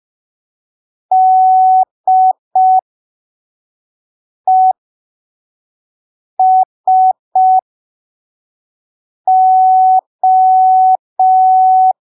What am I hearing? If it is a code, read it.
Morse code, heard as a keyed tone, about 5 wpm, spaced as for 3 wpm, 750 Hz. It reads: DESO
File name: arquivos / texto / texto 05 5ppm